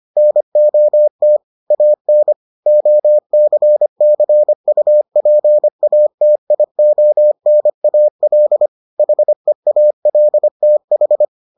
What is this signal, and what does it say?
Morse code, heard as a keyed tone, about 25 wpm, 600 Hz. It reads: NOT AN OCCUPATIONAL HEALTH